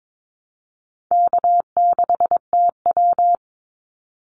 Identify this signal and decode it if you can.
Morse code, heard as a keyed tone, about 22 wpm, 700 Hz. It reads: K6TW
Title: NA SOTA Activator Callsigns 22wpm